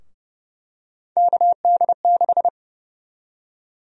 Morse code, heard as a keyed tone, about 32 wpm, 700 Hz. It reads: KD6
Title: USA Call Sign Prefixes - Repeated 2x - 30wpm